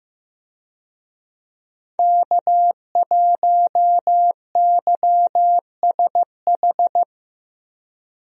Morse code, heard as a keyed tone, about 16 wpm, 700 Hz. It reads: K1YSH
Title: USA Call Signs - 5 Characters - Repeated 2x - 15wpm